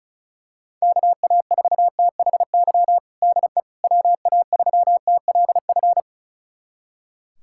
Morse code, heard as a keyed tone, about 36 wpm, 700 Hz. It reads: KA4THY DE WA3TLF
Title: USA Callsign DE Callsign 35wpm